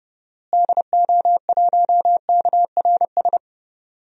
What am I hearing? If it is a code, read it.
Morse code, heard as a keyed tone, about 29 wpm, 700 Hz. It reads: DO1KRS
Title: DXCC-100 Call Signs - 6 Characters - Repeated 3x - 30wpm